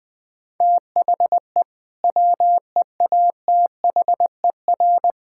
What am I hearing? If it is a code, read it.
Morse code, heard as a keyed tone, about 20 wpm, 700 Hz. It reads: THE WEATHER
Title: Sentences from Top 1000 Words - Review Part 5 - 20wpm